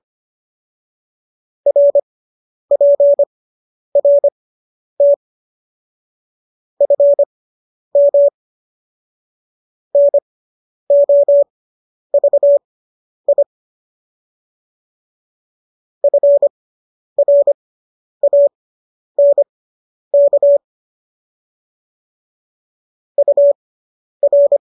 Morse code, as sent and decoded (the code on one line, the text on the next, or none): .-. .--. .-. - / ..-. -- / -. --- ...- .. / ..-. .-. .- -. -.- / ..- .-.
RPRT FM NOVI FRANK UR